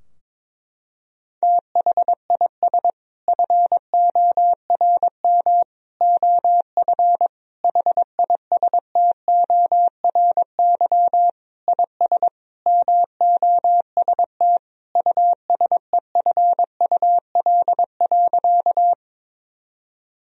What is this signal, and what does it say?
Morse code, heard as a keyed tone, about 22 wpm, 700 Hz. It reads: THIS FORM OF HISTORY IS MOST USEFUL.
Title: Sentences from Top 1000 Words - Review Part 4 - 22wpm